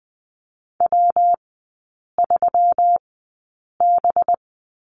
Morse code, heard as a keyed tone, about 19 wpm, spaced as for 8 wpm, 700 Hz. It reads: W3B
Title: 184 Lesson 30e - Letter b - Three Characters 20wpm